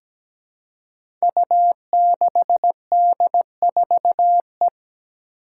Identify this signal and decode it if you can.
Morse code, heard as a keyed tone, about 17 wpm, 700 Hz. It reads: U6D4E
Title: Five Challenging Letter-Number Combinations 17wpm